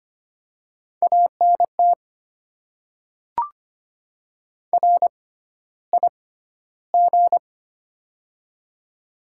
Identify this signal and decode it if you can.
Morse code, heard as a keyed tone, about 25 wpm, spaced as for 13 wpm, 700 Hz. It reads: ANT RIG
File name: Top 20 QSO Elements - 6x Character Spacing - 25wpm